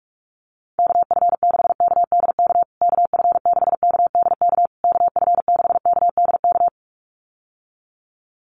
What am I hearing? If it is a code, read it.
Morse code, heard as a keyed tone, about 44 wpm, 690 Hz. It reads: KF6KDK KF6KDK KF6KDK
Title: USA Call Signs Repeated 3x 45wpm